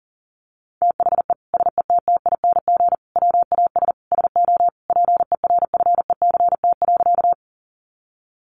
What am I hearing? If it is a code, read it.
Morse code, heard as a keyed tone, about 40 wpm, 700 Hz. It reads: THE SETTING WAS SO PERFECT.
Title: Sentences from Top 1300 Words 40wpm